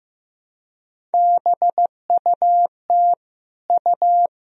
Morse code, heard as a keyed tone, about 15 wpm, 700 Hz. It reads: BUT U
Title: Sentences from Top 200 Words and Abbreviations 15wpm